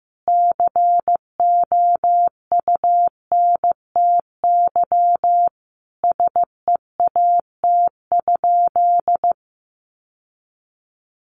Morse code, heard as a keyed tone, about 15 wpm, 700 Hz. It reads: COUNTY SEAT?
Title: Sentences from Top 1500 Words 15wpm